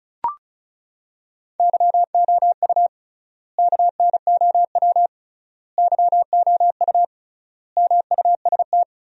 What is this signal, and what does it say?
Morse code, heard as a keyed tone, about 34 wpm, 700 Hz. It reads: YOU KNOW YOU MUST
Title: 3x Word-Spacing - Sentences from Top 500 Words - 35wpm